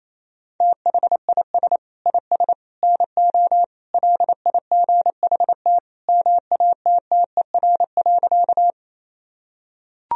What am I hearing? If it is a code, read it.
Morse code, heard as a keyed tone, about 28 wpm, 700 Hz. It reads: THIS IS NO LIGHT MATTER.
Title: Sentences from Top 1000 Words - Review Part 1 - 28wpm